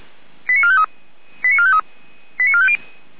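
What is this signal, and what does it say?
Five-tone selective call (ZVEI), heard as quick runs of five tones, tones of about 70 ms. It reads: 89453 89453 89450
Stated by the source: Five-tone ZVEI selective call received on an SDR.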